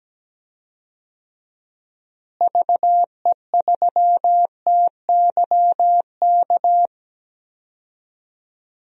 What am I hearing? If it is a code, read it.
Morse code, heard as a keyed tone, about 17 wpm, 700 Hz. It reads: VE3TYK
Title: DXCC-100 Call Signs - 6 Characters - Repeated 3x - 17wpm